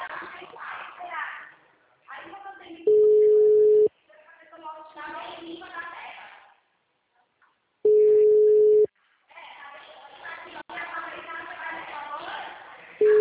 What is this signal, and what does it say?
Ringback tone (Europe, 425 Hz), with speech.